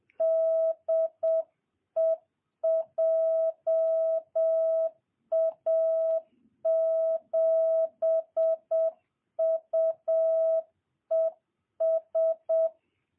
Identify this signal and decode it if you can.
Morse code, heard as a keyed tone, about 7 wpm, 650 Hz. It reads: DE JA7UES
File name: jh8cbh / 2022cw